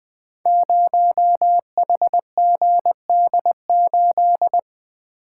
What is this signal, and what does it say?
Morse code, heard as a keyed tone, about 20 wpm, 700 Hz. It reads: 0HGD8
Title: Code Groups - 5 Letter:Numbers - 20wpm